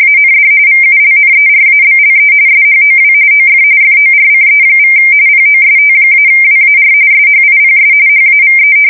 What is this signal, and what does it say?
RTTY, about 45.45 baud. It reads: AND VULNERABLE SHOW OF FORCE= POST-WAR SHIPS HAVE
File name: fsk170 / battleshipnj